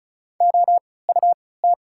Morse code, heard as a keyed tone, about 36 wpm, spaced as for 20 wpm, 700 Hz. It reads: OUT
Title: 3x Character Spacing - Top 100 Words - 35wpm